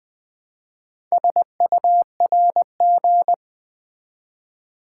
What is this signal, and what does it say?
Morse code, heard as a keyed tone, about 20 wpm, 700 Hz. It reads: SURG
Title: Build A Word - 2 Letters At A Time - 8-Letter Words - 20wpm